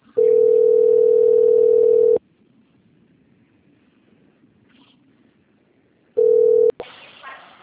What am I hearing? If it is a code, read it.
Ringback tone (North America), 440 + 480 Hz.